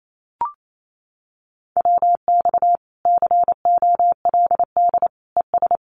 Morse code, heard as a keyed tone, about 28 wpm, 700 Hz. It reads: WX COLD ES
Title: Common QSO Phrases Encore 28wpm